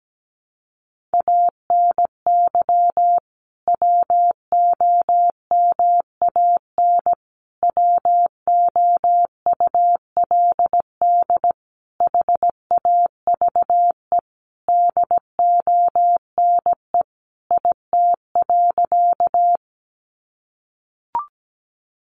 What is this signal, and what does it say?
Morse code, heard as a keyed tone, about 17 wpm, 700 Hz. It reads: ANY WOMAN WOULD HAVE DONE IT.
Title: Sentences from Top 1100 Words 17wpm